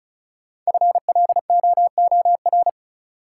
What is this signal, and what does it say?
Morse code, heard as a keyed tone, about 36 wpm, 700 Hz. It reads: FLOOR
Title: Sets of 3 Words - New Word Drill - 35wpm